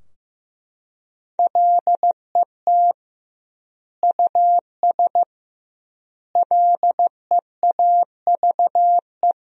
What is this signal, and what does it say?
Morse code, heard as a keyed tone, about 15 wpm, 700 Hz. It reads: LET US LEAVE
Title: 2x Word-Spacing - Sentences from Top 400 Words - 15wpm